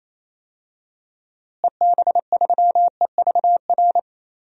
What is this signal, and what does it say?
Morse code, heard as a keyed tone, about 28 wpm, 700 Hz. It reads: EB3EVR
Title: DXCC-100 Call Signs - 6 Characters - Repeated 3x - 28wpm